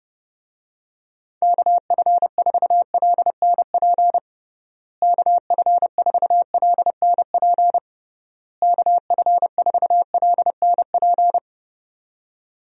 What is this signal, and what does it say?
Morse code, heard as a keyed tone, about 30 wpm, 700 Hz. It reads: KF4LNP KF4LNP KF4LNP
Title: USA Call Signs - 6 Characters - Repeated 3x - 30wpm